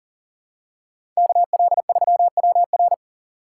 Morse code, heard as a keyed tone, about 40 wpm, 700 Hz. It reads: KL3WR
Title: USA Call Signs - 5 Characters - 6x Character Spacing - 40wpm